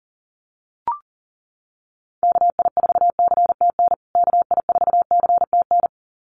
Morse code, heard as a keyed tone, about 40 wpm, 700 Hz. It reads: KI4CTN KI4CTN
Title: USA Call Signs Repeated 2x 40wpm